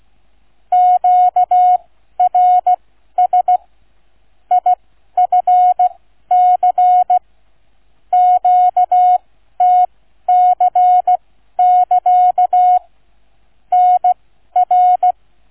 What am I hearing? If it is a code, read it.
Morse code, heard as a keyed tone, about 16 wpm, spaced as for 12 wpm, 730 Hz. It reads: QRS IFC QTC<KA> NR